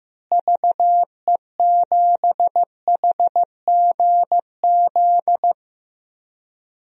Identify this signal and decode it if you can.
Morse code, heard as a keyed tone, about 15 wpm, 700 Hz. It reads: VE7HGZ
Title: DXCC-100 Call Signs - 6 Characters - Repeated 2x - 15wpm